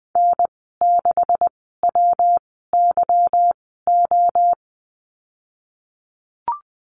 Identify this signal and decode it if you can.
Morse code, heard as a keyed tone, about 20 wpm, spaced as for 15 wpm, 700 Hz. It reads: N6WYO
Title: 147 Lesson 24g - Letter y - Call Signs 20wpm